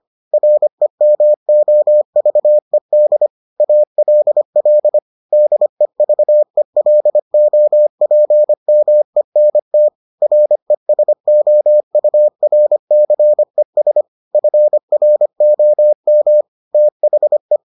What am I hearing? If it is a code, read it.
Morse code, heard as a keyed tone, about 25 wpm, 600 Hz. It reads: REMOVED ALL DEVELOPMENT RESOURCES FROM THE